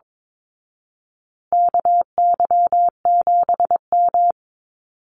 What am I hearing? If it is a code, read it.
Morse code, heard as a keyed tone, about 22 wpm, 700 Hz. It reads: KY7M
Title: Call Signs Repeated 3x - 22wpm